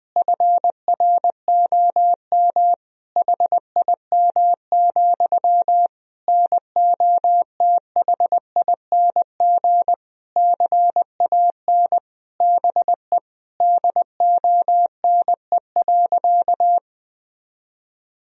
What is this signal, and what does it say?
Morse code, heard as a keyed tone, about 20 wpm, 700 Hz. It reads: FROM HIM, NOTHING CAN BE DONE.
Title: Sentences from Top 1000 Words - Review Part 1 - 20wpm